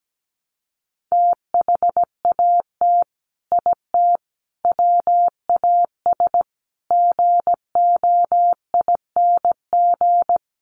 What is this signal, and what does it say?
Morse code, heard as a keyed tone, about 17 wpm, 700 Hz. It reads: THAT IT WAS GOING
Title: Sets of 5 Words Encore2 17wpm